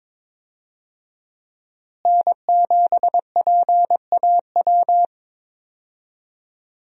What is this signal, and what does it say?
Morse code, heard as a keyed tone, about 21 wpm, 700 Hz. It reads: N7PAW
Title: USA Call Signs - 5 Characters - Repeated 3x - 22wpm